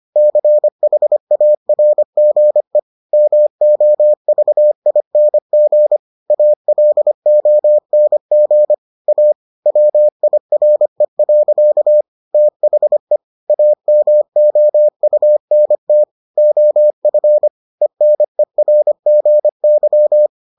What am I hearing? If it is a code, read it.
Morse code, heard as a keyed tone, about 25 wpm, 600 Hz. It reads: CHARGE MOVING ALONG A WIRE. THE AMOUNT OF ENERGY